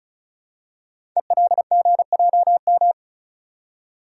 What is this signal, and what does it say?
Morse code, heard as a keyed tone, about 36 wpm, 700 Hz. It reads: ELGJM
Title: Code Groups - 5 Letters - 35wpm